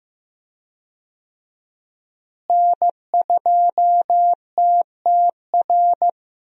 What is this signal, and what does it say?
Morse code, heard as a keyed tone, about 15 wpm, 700 Hz. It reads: N2TTR
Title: USA Call Signs - 5 Characters - Repeated 2x - 15wpm